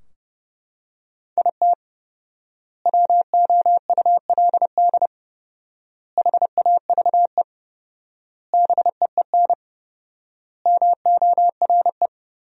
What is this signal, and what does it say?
Morse code, heard as a keyed tone, about 30 wpm, 700 Hz. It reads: IT WOULD HAVE BEEN MORE
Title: 4x Word-Spacing - Sentences from Top 400 Words - 30wpm